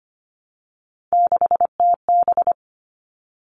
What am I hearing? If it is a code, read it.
Morse code, heard as a keyed tone, about 25 wpm, 700 Hz. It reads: 6TB
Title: All Cut Numbers - 3-Digit 25wpm